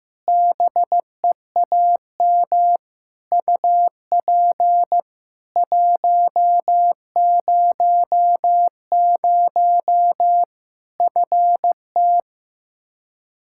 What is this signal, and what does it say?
Morse code, heard as a keyed tone, about 15 wpm, 700 Hz. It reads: BEAM UP 100 FT
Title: Practice QSO - Exchange 2 - Info - 15wpm